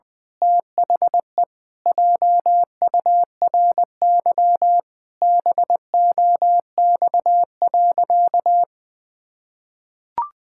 Morse code, heard as a keyed tone, about 20 wpm, 700 Hz. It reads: THE JURY BOX.
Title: Pangram Challenge v2 - 20wpm